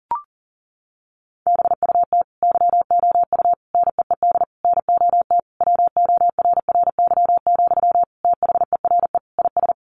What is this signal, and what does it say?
Morse code, heard as a keyed tone, about 40 wpm, 700 Hz. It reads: BUT YOU NEED NOT WORRY, THERE IS
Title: Sentences from Top 1300 Words 40wpm